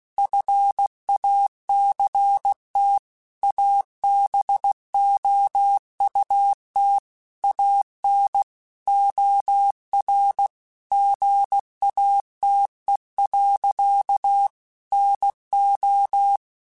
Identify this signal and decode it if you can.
Morse code, heard as a keyed tone, about 16 wpm, 800 Hz. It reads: FACT ABOUT AN OR GATE. NO
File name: Technical Subjects / Technical Subjects-mp3s / OR Gate @15